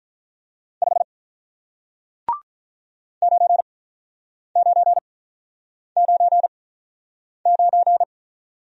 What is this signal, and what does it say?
Morse code, heard as a keyed tone, about 47 wpm, spaced as for 9 wpm, 700 Hz. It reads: 5 9999